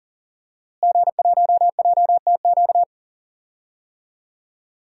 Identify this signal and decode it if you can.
Morse code, heard as a keyed tone, about 40 wpm, 700 Hz. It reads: G1JTQ
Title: DXCC-100 Call Signs - 5 Characters - Repeated 3x - 40wpm